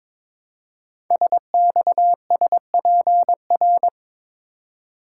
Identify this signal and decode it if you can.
Morse code, heard as a keyed tone, about 22 wpm, 700 Hz. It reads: SXSPR
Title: Code Groups - 5 Letters - 22wpm